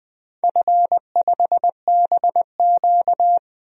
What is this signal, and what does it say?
Morse code, heard as a keyed tone, about 20 wpm, 700 Hz. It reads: F5BQ
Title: DXCC-100 Call Signs - 4 Characters - Repeated 2x - 20wpm